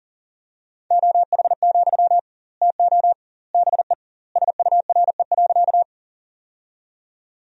Morse code, heard as a keyed tone, about 40 wpm, 700 Hz. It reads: OH, TO BE SURE.
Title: 2x Word-Spacing - Sentences from Top 500 Words - 40wpm